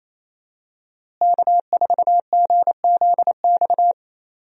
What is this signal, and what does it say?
Morse code, heard as a keyed tone, about 28 wpm, 700 Hz. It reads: K4GZX
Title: USA Call Signs - 5 Characters - Repeated 2x - 28wpm